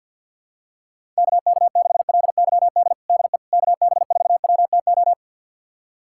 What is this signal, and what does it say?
Morse code, heard as a keyed tone, about 50 wpm, 700 Hz. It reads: KK6LYD DE KB4WTO
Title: USA Callsign DE Callsign - 6 Characters - 50wpm